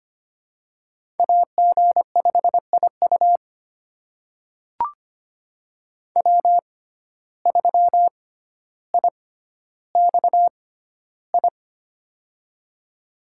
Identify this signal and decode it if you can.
Morse code, heard as a keyed tone, about 25 wpm, spaced as for 13 wpm, 700 Hz. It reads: AG5IU W3IXI
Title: USA Call Signs - 5 Characters - 6x Character Spacing - 25wpm